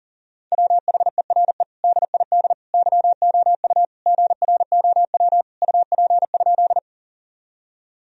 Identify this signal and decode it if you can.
Morse code, heard as a keyed tone, about 40 wpm, 700 Hz. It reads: WHERE DID YOU GROW UP?
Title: Sentences from Top 1000 Words - Review Part 1 - 40wpm